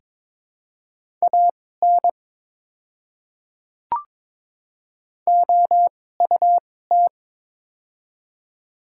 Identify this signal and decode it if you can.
Morse code, heard as a keyed tone, about 22 wpm, spaced as for 16 wpm, 700 Hz. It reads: AN OUT
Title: 2x Character Spacing - Top 100 Words - 22wpm